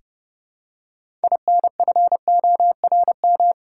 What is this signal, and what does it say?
Morse code, heard as a keyed tone, about 30 wpm, 700 Hz. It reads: INFORM
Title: Six Letter Words - 30wpm